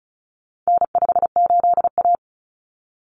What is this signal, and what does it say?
Morse code, heard as a keyed tone, about 34 wpm, 700 Hz. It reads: N58A
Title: Common Cut Numbers - 4-Digit 35wpm